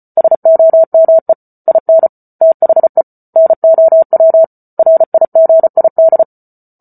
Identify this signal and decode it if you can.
Morse code, heard as a keyed tone, about 34 wpm, 650 Hz. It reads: SOME IN THE NOW RIGID